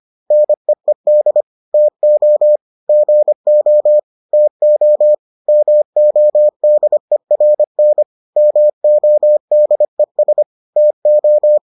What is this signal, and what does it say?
Morse code, heard as a keyed tone, about 25 wpm, 600 Hz. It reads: NEED TO GO TO MODERN MODES TO